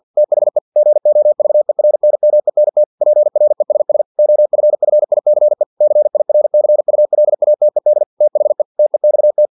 Morse code, heard as a keyed tone, about 50 wpm, 600 Hz. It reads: THE GOVERNMENT PRESS OFFICE CIRCULATED THE TEXT
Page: Never 49wpm